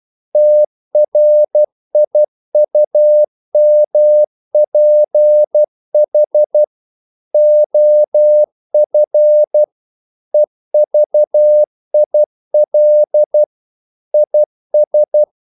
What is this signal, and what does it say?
Morse code, heard as a keyed tone, about 12 wpm, 600 Hz. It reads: TRIUMPH OF EVIL IS
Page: English quotations in CW (telegraphy - morse)